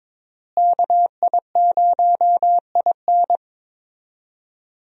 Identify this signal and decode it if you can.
Morse code, heard as a keyed tone, about 22 wpm, 700 Hz. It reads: KI0IN
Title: USA Call Signs - 5 Characters - Repeated 2x - 22wpm